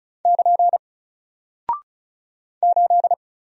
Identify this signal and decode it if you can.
Morse code, heard as a keyed tone, about 36 wpm, 700 Hz. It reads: <KN> 8
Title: Single Character (Letter:Number:Punctuation:Prosign) 35wpm